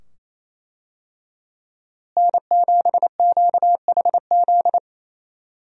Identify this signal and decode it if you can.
Morse code, heard as a keyed tone, about 28 wpm, 700 Hz. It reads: N7QHZ
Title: USA Call Signs - 5 Characters - Repeated 2x - 28wpm